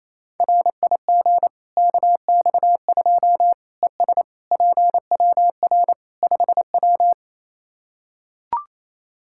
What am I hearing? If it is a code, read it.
Morse code, heard as a keyed tone, about 28 wpm, 700 Hz. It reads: RIG KX2 ES PWR 5W
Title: K1USN SST QSO Practice 28wpm